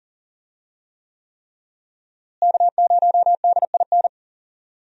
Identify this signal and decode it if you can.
Morse code, heard as a keyed tone, about 40 wpm, 700 Hz. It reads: K0DIN
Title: USA Call Signs - 5 Characters - Repeated 3x - 40wpm